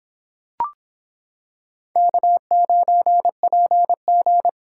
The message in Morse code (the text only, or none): K9PG